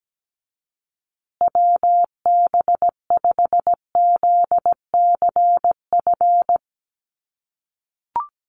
Morse code, heard as a keyed tone, about 17 wpm, 700 Hz. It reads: WB5ZCF
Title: USA Call Signs Repeated 3x 17wpm